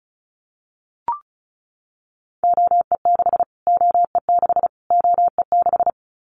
Morse code, heard as a keyed tone, about 34 wpm, 700 Hz. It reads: OE6 OE6 OE6
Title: Callsign Prefixes for Top 25 DXCC - Repeated 3x - 35wpm